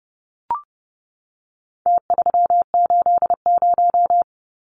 Morse code, heard as a keyed tone, about 30 wpm, 700 Hz. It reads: T380
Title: Numbers - Contest Style - 4-Digit 30wpm